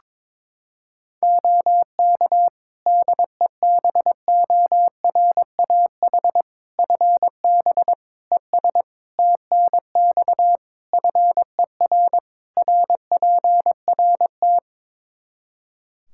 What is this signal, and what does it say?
Morse code, heard as a keyed tone, about 22 wpm, 700 Hz. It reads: OK DEBORAH FB ES TNX FER RPRT
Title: Practice QSO - Exchange 2 - Info - 22wpm